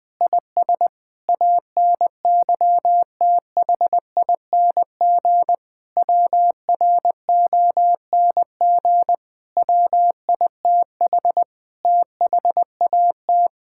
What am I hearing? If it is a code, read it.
Morse code, heard as a keyed tone, about 20 wpm, 700 Hz. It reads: IS ANYTHING WRONG WITH THAT
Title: Sentences with 7-Word Phrases - Spoken Beginning - 20wpm